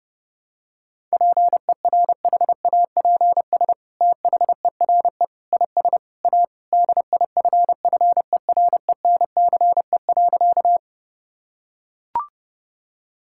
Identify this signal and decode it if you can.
Morse code, heard as a keyed tone, about 32 wpm, 700 Hz. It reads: PERHAPS THERE IS A DIFFERENCE.
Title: Sentences from Top 1000 Words - Review Part 5 - 30wpm